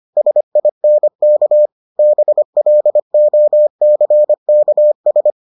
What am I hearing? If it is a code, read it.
Morse code, heard as a keyed tone, about 25 wpm, 600 Hz. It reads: SINK BLOCKS